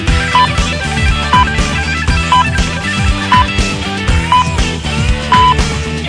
Radio time signal (time pips). It is heard from 0.3 to 5.5 s.